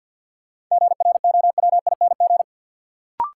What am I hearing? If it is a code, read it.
Morse code, heard as a keyed tone, about 53 wpm, 700 Hz. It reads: GROWING
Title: Sets of 5 Words Encore - New Word Drill - 50wpm